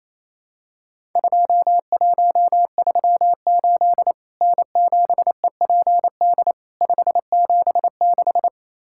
Morse code, heard as a keyed tone, about 28 wpm, 700 Hz. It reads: 2138 N7EPD 576
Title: Worked All Europe DX Contest - QTC Exchange - 28wpm